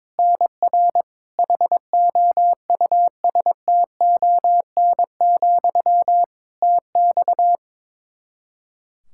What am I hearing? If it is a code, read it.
Morse code, heard as a keyed tone, about 22 wpm, 700 Hz. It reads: NR HOUSTON, TX
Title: K1USN SST QSO Practice 22wpm